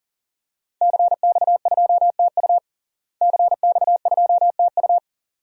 Morse code, heard as a keyed tone, about 40 wpm, 700 Hz. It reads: CX2TU CX2TU
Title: DXCC-100 Call Signs - 5 Characters - Repeated 2x - 40wpm